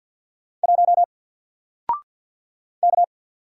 Morse code, heard as a keyed tone, about 48 wpm, 700 Hz. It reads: 1 K
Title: Single Letter-Number Alt 50wpm